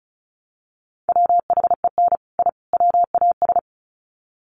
Morse code, heard as a keyed tone, about 34 wpm, 700 Hz. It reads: WHEN I WAS
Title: Sets of 3 Words 35wpm